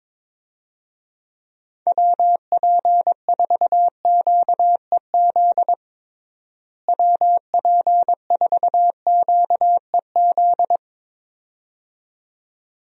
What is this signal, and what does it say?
Morse code, heard as a keyed tone, about 21 wpm, 700 Hz. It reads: WP4QEZ WP4QEZ
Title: USA Call Signs - 6 Characters - Repeated 2x - 22wpm